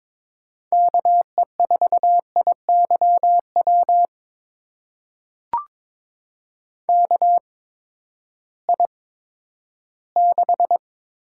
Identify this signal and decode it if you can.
Morse code, heard as a keyed tone, about 22 wpm, spaced as for 12 wpm, 700 Hz. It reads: KE4IYW KI6
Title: USA Call Signs - 6 Characters - 8x Character Spacing - 22wpm